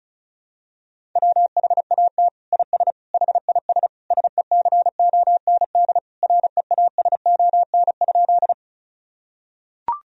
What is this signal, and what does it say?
Morse code, heard as a keyed tone, about 34 wpm, 700 Hz. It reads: WHAT IS HIS SECOND REASON?
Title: Sentences from Top 1000 Words - Review Part 1 - 35wpm